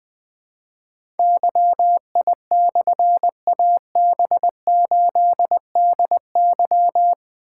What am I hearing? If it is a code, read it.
Morse code, heard as a keyed tone, about 20 wpm, 700 Hz. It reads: YI/AB8DY
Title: Difficult Call Signs 20wpm